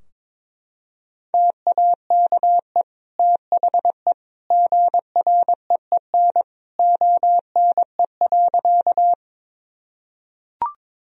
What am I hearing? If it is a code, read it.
Morse code, heard as a keyed tone, about 21 wpm, 700 Hz. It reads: TAKE THE GREEN ONE.
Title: Sentences from Top 1000 Words - Review Part 2 - 22wpm